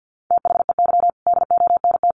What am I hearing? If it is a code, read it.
Morse code, heard as a keyed tone, about 50 wpm, 700 Hz. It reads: THEY DONT
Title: Sentences from Top 300 Words 50wpm